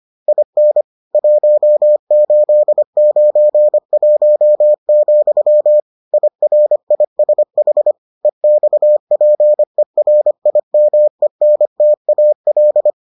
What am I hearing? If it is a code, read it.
Morse code, heard as a keyed tone, about 25 wpm, 600 Hz. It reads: IN 1891, IRISH EXPERIMENTAL